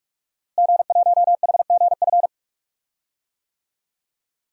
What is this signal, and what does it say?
Morse code, heard as a keyed tone, about 46 wpm, 700 Hz. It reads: G1HGF
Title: DXCC-100 Call Signs - 5 Characters - Repeated 2x - 45wpm